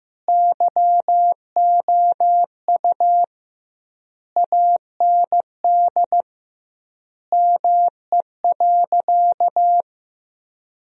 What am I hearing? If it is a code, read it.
Morse code, heard as a keyed tone, about 15 wpm, 700 Hz. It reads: YOU AND ME.
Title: 2x Word-Spacing - Sentences from Top 500 Words - 15wpm